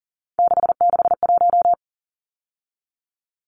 Morse code, heard as a keyed tone, about 40 wpm, 700 Hz. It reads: 661
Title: Numbers - 3-Digit 40wpm